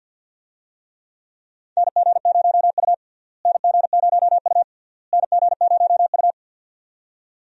Morse code, heard as a keyed tone, about 52 wpm, 700 Hz. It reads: NG0U NG0U NG0U
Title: USA Call Signs - 4 Characters - Repeated 3x - 50wpm